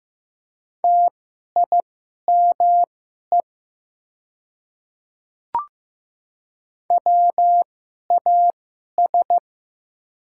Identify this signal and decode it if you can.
Morse code, heard as a keyed tone, about 15 wpm, spaced as for 11 wpm, 700 Hz. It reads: TIME WAS
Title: 2x Character Spacing - Top 200 Words - 15wpm